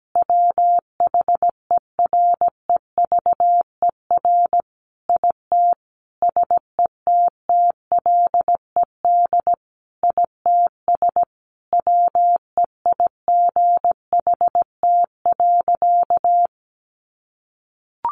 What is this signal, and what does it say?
Morse code, heard as a keyed tone, about 17 wpm, 700 Hz. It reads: WHEREVER IT SETTLED ITS WEIGHT.
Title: A Sound Of Thunder 17wpm